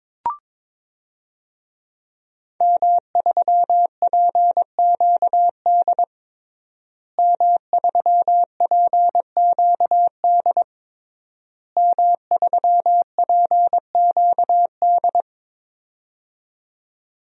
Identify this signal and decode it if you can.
Morse code, heard as a keyed tone, about 22 wpm, 700 Hz. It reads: M3PQD M3PQD M3PQD
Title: DXCC-100 Call Signs - 5 Characters - Repeated 3x - 22wpm